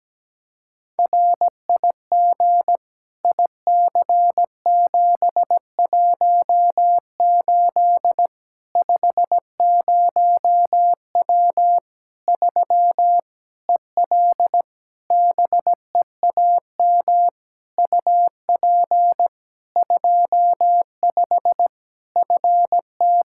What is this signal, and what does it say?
Morse code, heard as a keyed tone, about 17 wpm, 700 Hz. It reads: RIG IC718 50W 3 EL BEAM UP 25 FT
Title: Practice QSO - Exchange 2 - Info - 17wpm